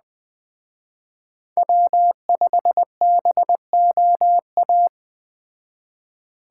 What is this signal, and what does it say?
Morse code, heard as a keyed tone, about 21 wpm, 700 Hz. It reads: W5BOA
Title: Speed Racing - USA 1x3 Call Signs 17wpm